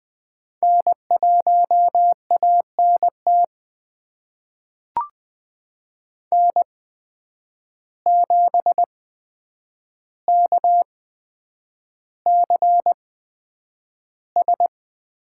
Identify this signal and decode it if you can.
Morse code, heard as a keyed tone, about 20 wpm, spaced as for 9 wpm, 700 Hz. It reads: N1ANT N7KCS
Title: USA Call Signs - 5 Characters - 8x Character Spacing - 20wpm